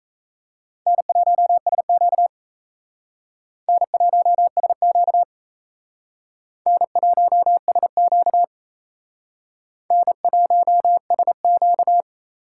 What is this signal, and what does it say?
Morse code, heard as a keyed tone, about 35 wpm, 700 Hz. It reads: N1SQ N1SQ N1SQ N1SQ
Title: Speed Racing - USA 1x2 Call Signs 28wpm